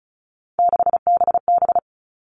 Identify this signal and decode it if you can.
Morse code, heard as a keyed tone, about 34 wpm, 700 Hz. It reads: 6BB
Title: All Cut Numbers - 3-Digit 35wpm